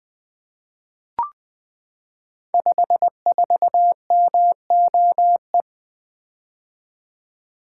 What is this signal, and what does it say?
Morse code, heard as a keyed tone, about 20 wpm, 700 Hz. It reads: 54MOE
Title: Code Groups - 5 Letter:Numbers - 20wpm